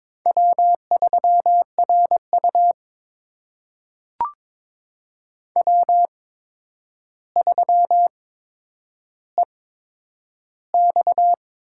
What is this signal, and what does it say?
Morse code, heard as a keyed tone, about 21 wpm, spaced as for 9 wpm, 700 Hz. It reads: W3RU W3EX
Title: USA Call Signs - 8x Character Spacing - 22wpm